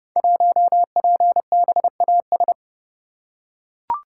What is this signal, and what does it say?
Morse code, heard as a keyed tone, about 30 wpm, 700 Hz. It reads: 1PBAS
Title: Code Groups - 5 Letter:Numbers - 30wpm